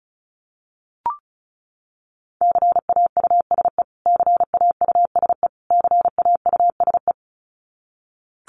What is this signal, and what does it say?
Morse code, heard as a keyed tone, about 34 wpm, 700 Hz. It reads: CAUSE CAUSE CAUSE
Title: Top 500 Words Repeated 35wpm